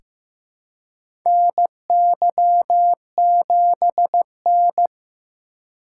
Morse code, heard as a keyed tone, about 15 wpm, 700 Hz. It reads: NY7N
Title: K1USN SST QSO Practice - 15wpm